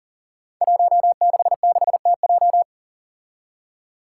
Morse code, heard as a keyed tone, about 40 wpm, 700 Hz. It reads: 166TJ
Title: Five Challenging Letter-Number Combinations 40wpm